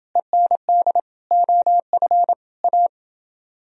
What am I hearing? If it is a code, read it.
Morse code, heard as a keyed tone, about 28 wpm, 700 Hz. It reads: END OF A